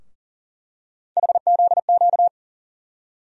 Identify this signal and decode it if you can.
Morse code, heard as a keyed tone, about 40 wpm, 700 Hz. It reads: HZQ
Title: USA Call Sign Postfixes - 3 Letters 40wpm